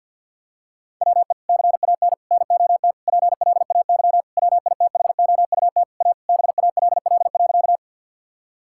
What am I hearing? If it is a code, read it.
Morse code, heard as a keyed tone, about 50 wpm, 700 Hz. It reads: WE CAN NOT PLAY WITHOUT A BALL.
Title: Sentences from Top 1000 Words - Review Part 6 - 50wpm